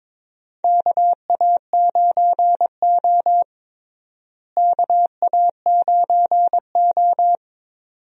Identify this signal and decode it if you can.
Morse code, heard as a keyed tone, about 22 wpm, 700 Hz. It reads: KA9O KA9O
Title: USA Call Signs - 4 Characters - Repeated 2x - 22wpm